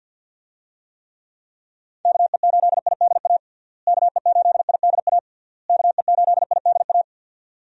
Morse code, heard as a keyed tone, about 51 wpm, 700 Hz. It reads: KE8IDA KE8IDA KE8IDA
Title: USA Call Signs - 6 Characters - Repeated 3x - 50wpm